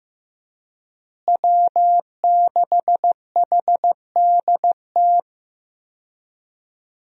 Morse code, heard as a keyed tone, about 15 wpm, 700 Hz. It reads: W6HDT
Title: USA Call Signs - 5 Characters - Repeated 3x - 15wpm